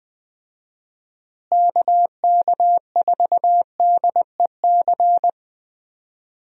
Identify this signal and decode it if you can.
Morse code, heard as a keyed tone, about 20 wpm, 700 Hz. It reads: KK4DEC
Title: USA Call Signs - 6 Characters - Repeated 2x - 20wpm